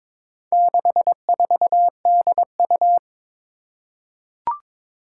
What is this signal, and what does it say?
Morse code, heard as a keyed tone, about 22 wpm, 700 Hz. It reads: 64DU
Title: Four Challenging Letter-Number Combinations 22wpm